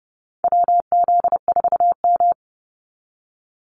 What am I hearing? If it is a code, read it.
Morse code, heard as a keyed tone, about 30 wpm, 700 Hz. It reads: WZ4M
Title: NA SOTA Activator Callsigns 30wpm